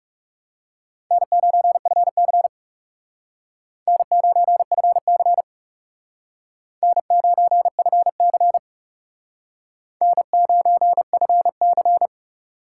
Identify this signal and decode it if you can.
Morse code, heard as a keyed tone, about 37 wpm, 700 Hz. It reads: N9FC N9FC N9FC N9FC
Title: Speed Racing - USA 1x2 Call Signs 30wpm